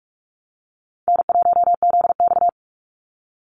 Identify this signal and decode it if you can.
Morse code, heard as a keyed tone, about 44 wpm, 700 Hz. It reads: N1ZX
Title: Easy Call Signs - 45wpm